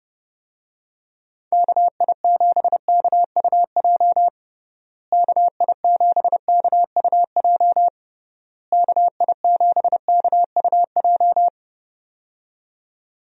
Morse code, heard as a keyed tone, about 30 wpm, 700 Hz. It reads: KI7KUJ KI7KUJ KI7KUJ
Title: USA Call Signs - 6 Characters - Repeated 3x - 30wpm